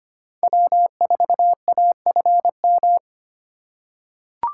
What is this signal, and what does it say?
Morse code, heard as a keyed tone, about 25 wpm, 700 Hz. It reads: W4AFM
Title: USA Call Signs - 5 Characters - 4x Character Spacing - 25wpm